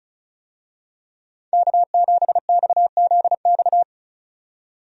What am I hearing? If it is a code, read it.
Morse code, heard as a keyed tone, about 36 wpm, 700 Hz. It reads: K7XZX
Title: USA Call Signs - 5 Characters - 35wpm